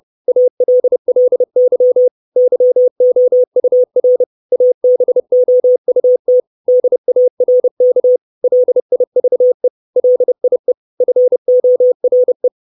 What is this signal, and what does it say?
Morse code, heard as a keyed tone, about 30 wpm, 500 Hz. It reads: ALLY YOUR ABOUT DARK LIVE LIE FORE
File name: 632 Words 30WPM